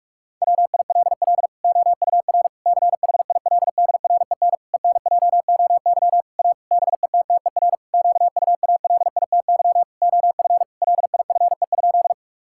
Morse code, heard as a keyed tone, about 46 wpm, 700 Hz. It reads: WILL OUR CHILDREN ENJOY A BETTER QUALITY OF LIFE?
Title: Sentences from Top 1000 Words - Review Part 5 - 45wpm